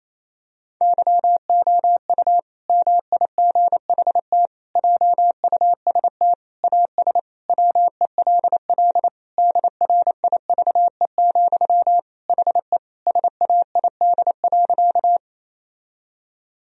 Morse code, heard as a keyed tone, about 28 wpm, 700 Hz. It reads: YOU MIGHT JUST AS WELL DRIVE, HE SAID.
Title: Sentences from Top 1000 Words - Review Part 2 - 28wpm